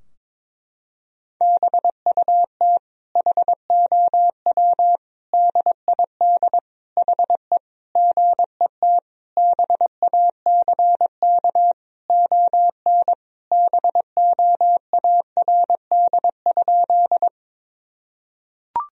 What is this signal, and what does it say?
Morse code, heard as a keyed tone, about 22 wpm, 700 Hz. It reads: BUT HOW DID HE GET BACK ON BOARD?
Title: Sentences from Top 1000 Words - Review Part 4 - 22wpm